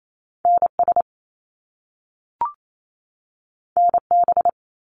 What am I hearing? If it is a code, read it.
Morse code, heard as a keyed tone, about 28 wpm, 700 Hz. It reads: NS NB
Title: Canadian Provinces and Territories 28wpm